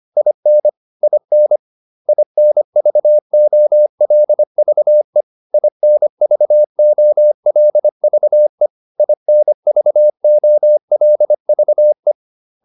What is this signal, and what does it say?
Morse code, heard as a keyed tone, about 25 wpm, 600 Hz. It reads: IN IN INVOLVE INVOLVE INVOLVE